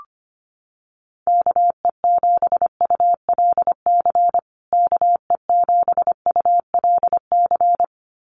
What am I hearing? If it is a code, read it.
Morse code, heard as a keyed tone, about 25 wpm, 700 Hz. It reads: KE7ULC KE7ULC
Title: USA Call Signs Repeated 2x 25wpm